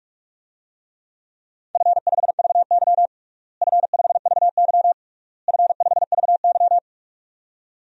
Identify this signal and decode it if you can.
Morse code, heard as a keyed tone, about 46 wpm, 700 Hz. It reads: F5VY F5VY F5VY
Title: DXCC-100 Call Signs - 4 Characters - Repeated 3x - 45wpm